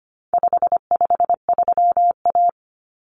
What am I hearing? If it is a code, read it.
Morse code, heard as a keyed tone, about 25 wpm, 700 Hz. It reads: 553A
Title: Common Cut Numbers - 4-Digit 25wpm